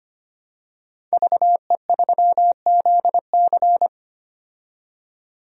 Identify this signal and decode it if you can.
Morse code, heard as a keyed tone, about 25 wpm, 700 Hz. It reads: VE3ZC
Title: DXCC-100 Call Signs - 5 Characters - 25wpm